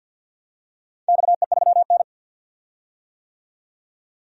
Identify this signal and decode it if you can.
Morse code, heard as a keyed tone, about 49 wpm, 700 Hz. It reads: XE3N
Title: DXCC-100 Call Signs - 4 Characters - 50wpm